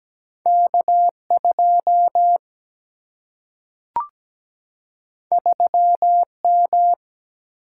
Morse code, heard as a keyed tone, about 17 wpm, 700 Hz. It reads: K2 3M
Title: Single-Letter Single-Number 17wpm